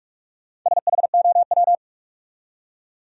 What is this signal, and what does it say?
Morse code, heard as a keyed tone, about 55 wpm, 700 Hz. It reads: SHOW
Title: Speed-Racing Top 200 Words 45wpm